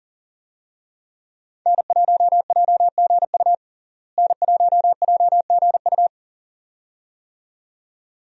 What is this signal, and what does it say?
Morse code, heard as a keyed tone, about 37 wpm, 700 Hz. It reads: N1JGU N1JGU
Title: USA Call Signs - 5 Characters - Repeated 2x - 40wpm